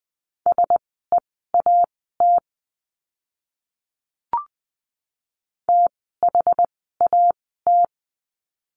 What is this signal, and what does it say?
Morse code, heard as a keyed tone, about 20 wpm, spaced as for 14 wpm, 700 Hz. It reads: SEAT THAT
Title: 061 Lesson 11f - Letter h - Words 20wpm